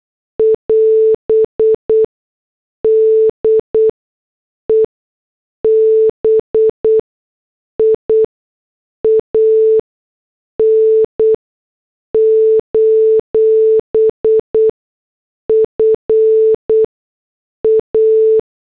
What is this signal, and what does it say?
Morse code, heard as a keyed tone, about 8 wpm, spaced as for 6 wpm, 440 Hz. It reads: <AS> DEBIAN: FA